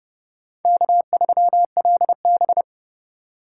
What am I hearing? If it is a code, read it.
Morse code, heard as a keyed tone, about 38 wpm, 700 Hz. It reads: K3LB